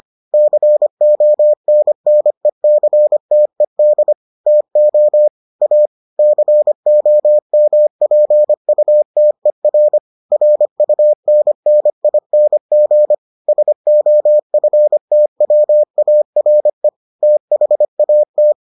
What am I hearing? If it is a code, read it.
Morse code, heard as a keyed tone, about 25 wpm, 600 Hz. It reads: CONNECTED TO A COMPUTER RUNNING SOFTWARE THAT